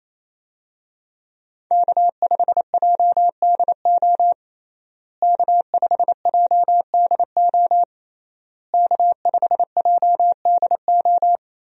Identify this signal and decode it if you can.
Morse code, heard as a keyed tone, about 28 wpm, 700 Hz. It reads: K5JDO K5JDO K5JDO
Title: USA Call Signs - 5 Characters - Repeated 3x - 28wpm